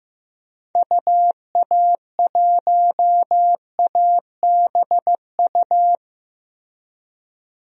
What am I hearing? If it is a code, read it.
Morse code, heard as a keyed tone, about 15 wpm, 700 Hz. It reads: UA1ABU
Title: DXCC-100 Call Signs - 6 Characters - Repeated 2x - 15wpm